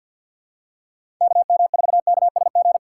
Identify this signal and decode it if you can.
Morse code, heard as a keyed tone, about 47 wpm, 700 Hz. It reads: KM4KSG
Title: USA Call Signs - 6 Characters - 4x Character Spacing - 50wpm